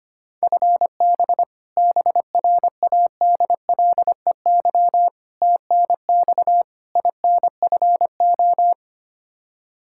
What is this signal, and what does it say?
Morse code, heard as a keyed tone, about 25 wpm, 700 Hz. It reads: FB BRADLEY TNX INFO
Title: Practice QSO - Exchange 3 - More Info - 25wpm